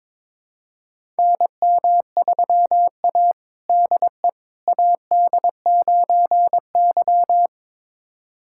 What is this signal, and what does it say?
Morse code, heard as a keyed tone, about 22 wpm, 700 Hz. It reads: NM3A DE AD9Y
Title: USA Callsign DE Callsign - 4 Characters - 22wpm